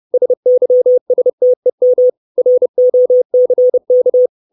Morse code, heard as a keyed tone, about 30 wpm, 500 Hz. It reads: SYSTEM ROCK
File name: 632 Words 30WPM